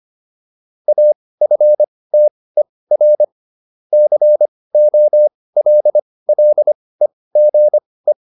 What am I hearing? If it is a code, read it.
Morse code, heard as a keyed tone, about 25 wpm, spaced as for 18 wpm, 600 Hz. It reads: AFTER COLLEGE